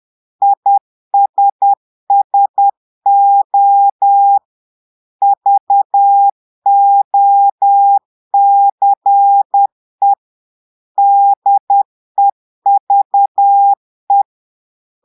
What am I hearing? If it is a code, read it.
Morse code, heard as a keyed tone, about 10 wpm, 800 Hz. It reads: ISSO VOCE DEVE
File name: portugais texte en clair 2 vitesse 10 mots minutes